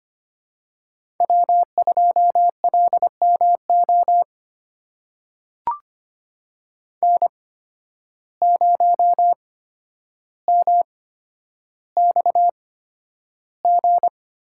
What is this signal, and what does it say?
Morse code, heard as a keyed tone, about 25 wpm, spaced as for 11 wpm, 700 Hz. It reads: W2LMO N0MXG
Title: USA Call Signs - 5 Characters - 8x Character Spacing - 25wpm